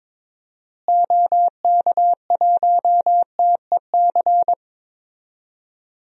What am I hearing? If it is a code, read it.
Morse code, heard as a keyed tone, about 22 wpm, 700 Hz. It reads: OK1TEC
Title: DXCC-100 Call Signs - 6 Characters - Repeated 3x - 22wpm